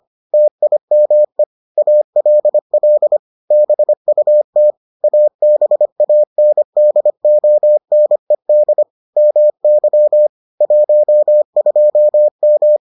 Morse code, heard as a keyed tone, about 25 wpm, 600 Hz. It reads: TIME ALL BUT ABANDONED MY 12M